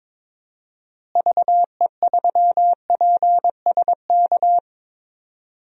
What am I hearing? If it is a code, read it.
Morse code, heard as a keyed tone, about 22 wpm, 700 Hz. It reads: VE3PSK
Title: DXCC-100 Call Signs - 6 Characters - Repeated 3x - 22wpm